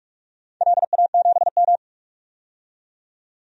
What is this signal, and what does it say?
Morse code, heard as a keyed tone, about 47 wpm, 700 Hz. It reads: LA7M
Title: DXCC-100 Call Signs - 4 Characters - 45wpm